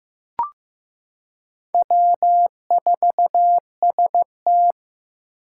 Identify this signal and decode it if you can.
Morse code, heard as a keyed tone, about 15 wpm, 700 Hz. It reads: W4ST
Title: Four Challenging Letter-Number Combinations 15wpm